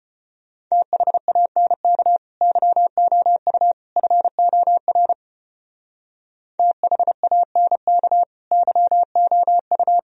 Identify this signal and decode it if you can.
Morse code, heard as a keyed tone, about 32 wpm, 700 Hz. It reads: THANK YOU FOR THANK YOU